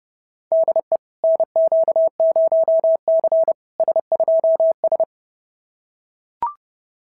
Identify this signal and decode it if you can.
Morse code, heard as a keyed tone, about 32 wpm, 690 Hz. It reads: DE NQ0C S2S
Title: SOTA QSO Practice 30wpm